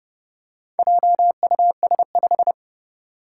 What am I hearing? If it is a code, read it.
Morse code, heard as a keyed tone, about 30 wpm, 700 Hz. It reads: JUS5
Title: Four Challenging Letter-Number Combinations 30wpm